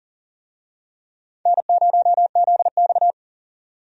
Morse code, heard as a keyed tone, about 40 wpm, 700 Hz. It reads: N0ZX
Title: USA Call Signs - 4 Characters - 40wpm